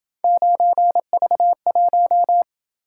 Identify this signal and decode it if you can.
Morse code, heard as a keyed tone, about 27 wpm, 700 Hz. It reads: 9V1